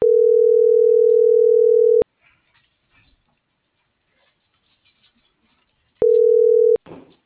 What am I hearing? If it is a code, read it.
Ringback tone (North America), 440 + 480 Hz.